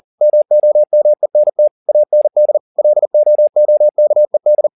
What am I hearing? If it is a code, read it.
Morse code, heard as a keyed tone, about 40 wpm, 600 Hz. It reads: MOMENT AND LOOKED
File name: the old man and the sea 2 40wpm